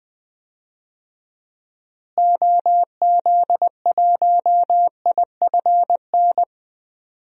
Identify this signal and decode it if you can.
Morse code, heard as a keyed tone, about 20 wpm, 700 Hz. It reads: OZ1IFN
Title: DXCC-100 Call Signs - 6 Characters - Repeated 3x - 20wpm